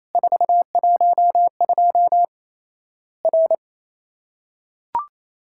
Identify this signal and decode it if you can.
Morse code, heard as a keyed tone, about 28 wpm, 700 Hz. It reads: 412 R
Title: Worked All Europe DX Contest - QTC Exchange - 28wpm